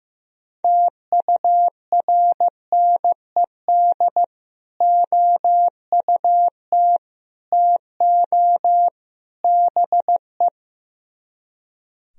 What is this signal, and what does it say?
Morse code, heard as a keyed tone, about 16 wpm, 700 Hz. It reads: TURNED OUT TO BE
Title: Speed-Racing - Sets of 4 Words 15wpm